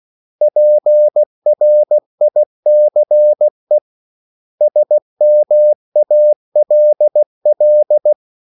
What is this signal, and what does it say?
Morse code, heard as a keyed tone, about 16 wpm, 600 Hz. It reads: PRICE SMALL
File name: CW / Practice / 500MostCommonEnglishWords16